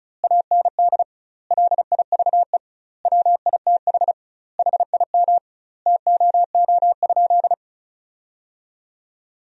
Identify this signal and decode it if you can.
Morse code, heard as a keyed tone, about 36 wpm, 700 Hz. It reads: AND LIVE WITH HIM TOO?
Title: 2x Word-Spacing - Sentences from Top 500 Words - 35wpm